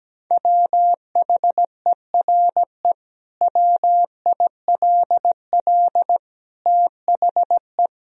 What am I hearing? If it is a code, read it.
Morse code, heard as a keyed tone, about 17 wpm, 700 Hz. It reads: WHERE WILL THE
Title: Sentences from Top 1000 Words - Review Part 4 - 17wpm